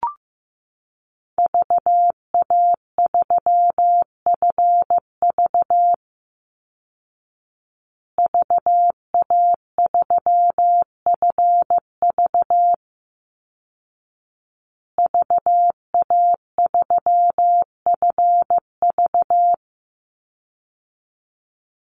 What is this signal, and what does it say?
Morse code, heard as a keyed tone, about 15 wpm, 700 Hz. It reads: VA3FV VA3FV VA3FV
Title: Call Signs Repeated 2x - 15wpm